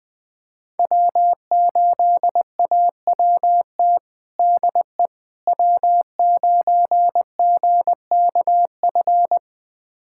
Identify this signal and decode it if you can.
Morse code, heard as a keyed tone, about 20 wpm, 700 Hz. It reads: W8AWT DE W9GKF
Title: USA Callsign DE Callsign - 5 Characters - 20wpm